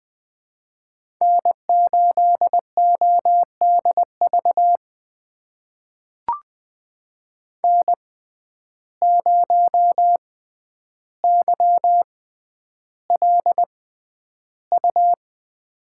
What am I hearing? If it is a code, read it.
Morse code, heard as a keyed tone, about 20 wpm, spaced as for 10 wpm, 700 Hz. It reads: N8ODV N0YLU
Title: USA Call Signs - 5 Characters - 6x Character Spacing - 20wpm